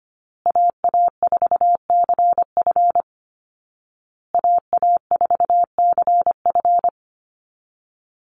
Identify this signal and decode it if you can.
Morse code, heard as a keyed tone, about 25 wpm, 700 Hz. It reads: AA4CF AA4CF
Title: Call Signs Repeated 2x - 25wpm